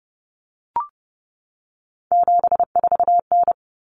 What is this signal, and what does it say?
Morse code, heard as a keyed tone, about 30 wpm, 700 Hz. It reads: 74N
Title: Common Cut Numbers - 3-Digit 30wpm